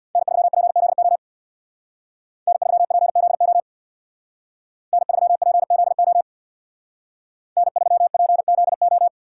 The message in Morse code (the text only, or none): N3PZO N3PZO N3PZO N3PZO